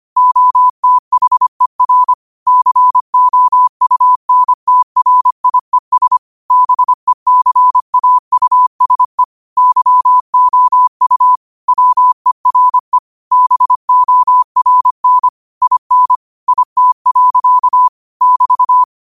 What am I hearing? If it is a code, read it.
Morse code, heard as a keyed tone, about 25 wpm, 1000 Hz. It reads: OTHER COUNTRIES BECAUSE YOU WERE BORN IN IT. <BT>
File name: qotdfiles / 25wpm